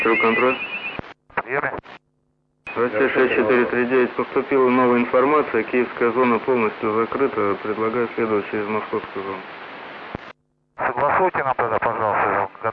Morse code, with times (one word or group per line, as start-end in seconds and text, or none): none